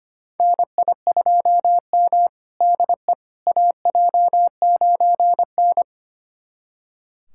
Morse code, heard as a keyed tone, about 25 wpm, 700 Hz. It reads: NI2M DE AJ9N
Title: USA Callsign DE Callsign - 4 Characters - 25wpm